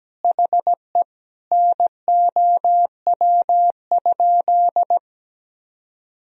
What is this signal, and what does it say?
Morse code, heard as a keyed tone, about 17 wpm, 700 Hz. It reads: HE NOW?
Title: Sentences from Top 200 Words and Abbreviations 17wpm